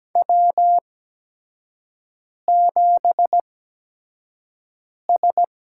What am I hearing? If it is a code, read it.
Morse code, heard as a keyed tone, about 17 wpm, spaced as for 5 wpm, 700 Hz. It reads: W7S
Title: USA Call Signs - 8x Character Spacing - 17wpm